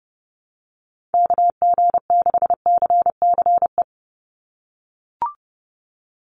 Morse code, heard as a keyed tone, about 30 wpm, 700 Hz. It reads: KG6CCE
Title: USA Call Signs 30wpm